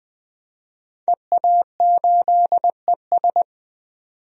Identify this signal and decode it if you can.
Morse code, heard as a keyed tone, about 20 wpm, 700 Hz. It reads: EA8ES
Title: DXCC-100 Call Signs - 5 Characters - 20wpm